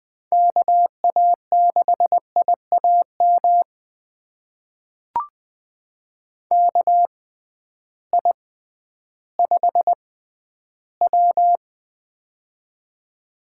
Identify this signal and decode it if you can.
Morse code, heard as a keyed tone, about 20 wpm, spaced as for 12 wpm, 700 Hz. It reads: KA6IAM KI5W
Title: USA Call Signs - 6x Character Spacing - 20wpm